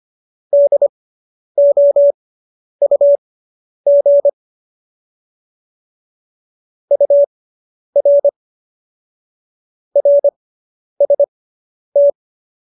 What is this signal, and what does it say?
Morse code, heard as a keyed tone, about 25 wpm, spaced as for 10 wpm, 570 Hz. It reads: DOUG UR RST
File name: QSO220 10